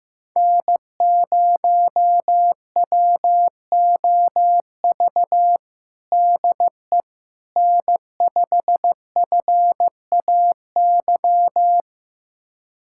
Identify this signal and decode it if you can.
Morse code, heard as a keyed tone, about 15 wpm, 700 Hz. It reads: N0WOV DE N5FAY
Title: USA Callsign DE Callsign - 5 Characters - 15wpm